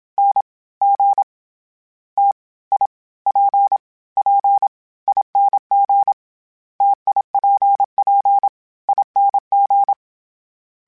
Morse code, heard as a keyed tone, about 27 wpm, 800 Hz. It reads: N G T I P P ING TIPP ING
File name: Variable Spacing / Variable Spacing-mp3s / T AOEs @25